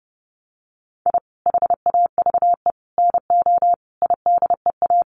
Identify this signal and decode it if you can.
Morse code, heard as a keyed tone, about 30 wpm, 700 Hz. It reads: I HAVE NO IDEA
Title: Sets of 5 Words Encore 30wpm